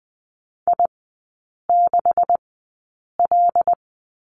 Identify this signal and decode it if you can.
Morse code, heard as a keyed tone, about 20 wpm, spaced as for 8 wpm, 700 Hz. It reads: I6L
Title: 172 Lesson 28e - Number 9 - Three Characters 20wpm